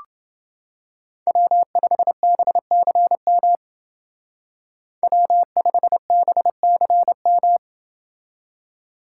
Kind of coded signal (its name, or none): Morse code